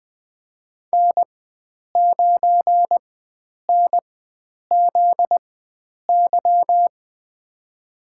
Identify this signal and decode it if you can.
Morse code, heard as a keyed tone, about 20 wpm, spaced as for 9 wpm, 700 Hz. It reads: N9NZY
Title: USA Call Signs - 5 Characters - 4x Character Spacing - 20wpm